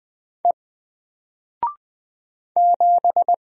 Morse code, heard as a keyed tone, about 20 wpm, 700 Hz. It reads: E 7
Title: Single Letter-Number Alt 20wpm